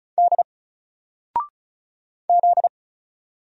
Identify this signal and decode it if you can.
Morse code, heard as a keyed tone, about 36 wpm, 700 Hz. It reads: D Z
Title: Single Character (Letter:Number:Punctuation:Prosign) 35wpm